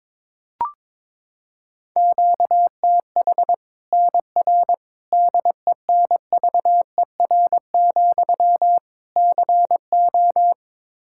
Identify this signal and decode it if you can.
Morse code, heard as a keyed tone, about 22 wpm, 700 Hz. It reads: QTH NR DENVER, CO
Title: Practice QSO - Exchange 1 - Beginning - 22wpm